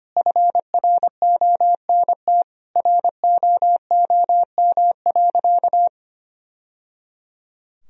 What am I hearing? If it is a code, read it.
Morse code, heard as a keyed tone, about 25 wpm, 700 Hz. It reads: FRONT ROOM.
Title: Sentences from Top 1000 Words - Review Part 2 - 25wpm